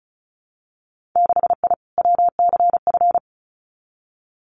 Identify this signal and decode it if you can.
Morse code, heard as a keyed tone, about 36 wpm, 700 Hz. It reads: 6I WCF
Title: Winter Field Day Exchanges Encore 35wpm